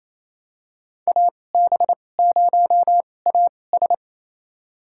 Morse code, heard as a keyed tone, about 28 wpm, spaced as for 20 wpm, 700 Hz. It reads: AB0AS
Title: USA Call Signs - 5 Characters - 2x Character Spacing - 28wpm